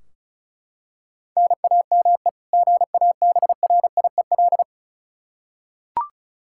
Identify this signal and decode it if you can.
Morse code, heard as a keyed tone, about 36 wpm, 700 Hz. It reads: NAME GABRIEL
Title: K1USN SST QSO Practice 35wpm